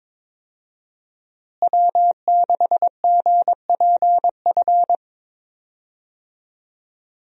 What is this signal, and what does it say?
Morse code, heard as a keyed tone, about 22 wpm, 700 Hz. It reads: W6GPF
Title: USA Call Signs - 5 Characters - Repeated 3x - 22wpm